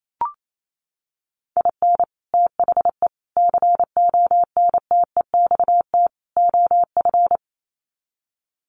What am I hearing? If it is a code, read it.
Morse code, heard as a keyed tone, about 28 wpm, 700 Hz. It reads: IN THE CONTEXT OF
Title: Sets of 4 Words 28wpm